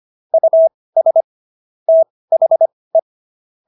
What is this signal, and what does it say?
Morse code, heard as a keyed tone, about 25 wpm, spaced as for 18 wpm, 650 Hz. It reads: US THE